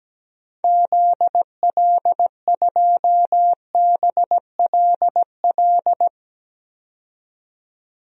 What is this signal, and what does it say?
Morse code, heard as a keyed tone, about 17 wpm, 700 Hz. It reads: ZL2BLL
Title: DXCC-100 Call Signs - 6 Characters - 17wpm